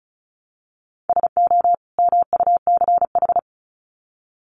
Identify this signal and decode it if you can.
Morse code, heard as a keyed tone, about 34 wpm, 700 Hz. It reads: SO MUCH
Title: Sets of 2 Words 35wpm